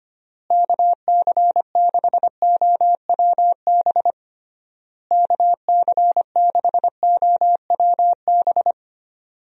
Morse code, heard as a keyed tone, about 25 wpm, 700 Hz. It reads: KC6OWB KC6OWB
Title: USA Call Signs - 6 Characters - Repeated 3x - 25wpm